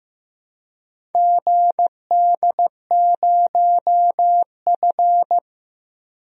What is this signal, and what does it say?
Morse code, heard as a keyed tone, about 16 wpm, 700 Hz. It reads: GD0F
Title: DXCC-100 Call Signs - 4 Characters - 15wpm